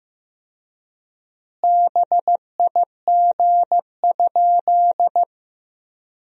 Morse code, heard as a keyed tone, about 15 wpm, 700 Hz. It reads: BIG?
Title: 3x Word-Spacing - Sentences from Top 200 Words - 15wpm